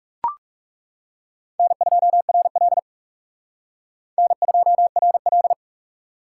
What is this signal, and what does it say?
Morse code, heard as a keyed tone, about 36 wpm, 700 Hz. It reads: N2RL N2RL
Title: Speed Racing - USA 1x2 Call Signs 30wpm